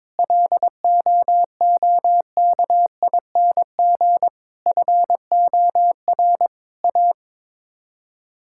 Morse code, heard as a keyed tone, about 22 wpm, 700 Hz. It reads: LOOKING FOR A
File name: practice / 22wpm / Sets of 6 Words - 22wpm